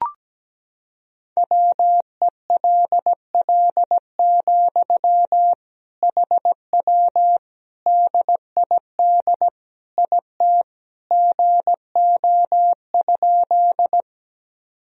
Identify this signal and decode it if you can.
Morse code, heard as a keyed tone, about 17 wpm, 700 Hz. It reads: WELL, HW DID IT GO?
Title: Sentences from Top 200 Words and Abbreviations 17wpm